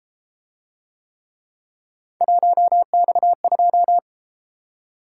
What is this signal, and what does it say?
Morse code, heard as a keyed tone, about 30 wpm, 700 Hz. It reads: 1X2